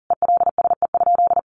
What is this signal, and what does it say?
Morse code, heard as a keyed tone, about 37 wpm, 700 Hz. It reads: ELSE?
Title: Sentences from Top 900 Words 40wpm